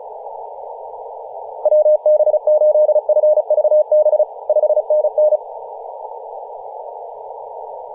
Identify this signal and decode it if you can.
Morse code, heard as a keyed tone, about 34 wpm, 610 Hz, with radio noise. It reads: WB8FVB 5NN
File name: posts / 2017-nov-04-2233z-vk9ma-21-025mhz-cw